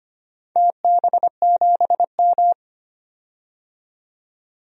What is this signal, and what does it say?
Morse code, heard as a keyed tone, about 25 wpm, 700 Hz. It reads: TB7M
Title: DXCC-100 Call Signs - 4 Characters - 25wpm